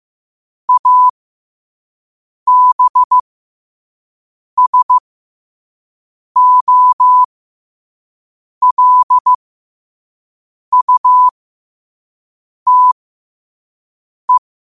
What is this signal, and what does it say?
Morse code, heard as a keyed tone, about 15 wpm, spaced as for 5 wpm, 1000 Hz. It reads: ABSOLUTE
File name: qotdfiles / 5wpm